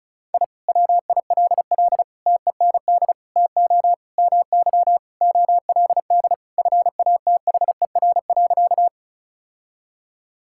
Morse code, heard as a keyed tone, about 34 wpm, 700 Hz. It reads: I WILL TEND TO MY OLD FATHER.
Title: Sentences from Top 1000 Words - Review Part 5 - 35wpm